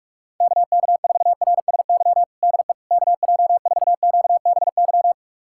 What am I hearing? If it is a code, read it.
Morse code, heard as a keyed tone, about 46 wpm, 700 Hz. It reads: KK4RSY DE KJ4QBY
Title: USA Callsign DE Callsign - 6 Characters - 45wpm